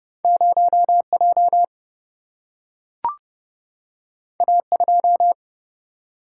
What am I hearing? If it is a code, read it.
Morse code, heard as a keyed tone, about 30 wpm, 700 Hz. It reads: 0 J A 2
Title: Single-Letter Single-Number 30wpm